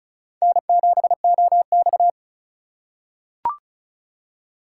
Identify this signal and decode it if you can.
Morse code, heard as a keyed tone, about 36 wpm, 690 Hz. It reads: N7OX
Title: USA Call Signs - 4 Characters - 3x Character Spacing - 35wpm